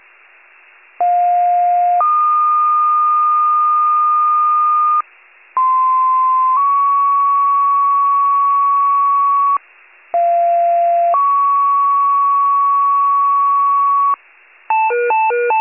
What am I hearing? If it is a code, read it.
Two-tone page, heard as a one-second tone, then a three-second tone, about 710 then 1220 Hz, about 1040 then 1120 Hz, about 690 then 1120 Hz. It is heard from 1.0 to 14.1 s.
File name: pager / 95 RESCUE / wcalert